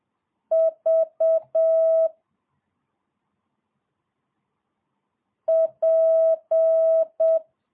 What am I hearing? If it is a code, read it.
Morse code, heard as a keyed tone, about 7 wpm, 650 Hz. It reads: V P
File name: jh8cbh / 2022cw